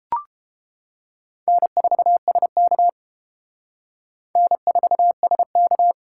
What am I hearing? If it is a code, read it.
Morse code, heard as a keyed tone, about 30 wpm, 700 Hz. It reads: N4SK N4SK